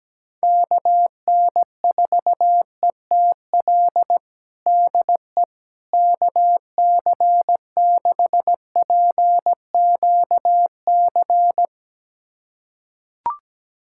Morse code, heard as a keyed tone, about 17 wpm, 700 Hz. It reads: KN4ETL DE KC6PQC
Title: USA Callsign DE Callsign 17wpm